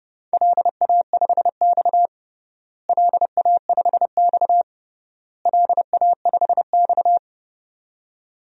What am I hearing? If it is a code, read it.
Morse code, heard as a keyed tone, about 30 wpm, 700 Hz. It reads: LA5X LA5X LA5X
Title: DXCC-100 Call Signs - 4 Characters - Repeated 3x - 30wpm